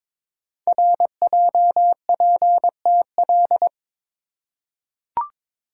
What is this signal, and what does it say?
Morse code, heard as a keyed tone, about 22 wpm, 700 Hz. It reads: RJPTL
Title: Code Groups - 5 Letters - 22wpm